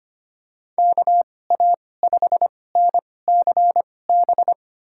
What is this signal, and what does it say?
Morse code, heard as a keyed tone, about 25 wpm, spaced as for 18 wpm, 700 Hz. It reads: KA5NCB
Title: USA Call Signs - 2x Character Spacing - 25wpm